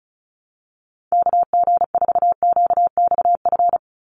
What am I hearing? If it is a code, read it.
Morse code, heard as a keyed tone, about 34 wpm, 700 Hz. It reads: KG4QXF
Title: USA Call Signs 35wpm